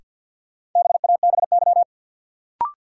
Morse code, heard as a keyed tone, about 53 wpm, 700 Hz. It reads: BABY
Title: Sentences from Top 1300 Words - New Word Drill Alt - 50wpm